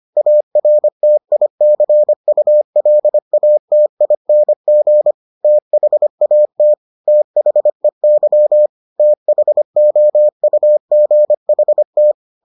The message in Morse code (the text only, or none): ARTICULATING THAT THEY THOUGHT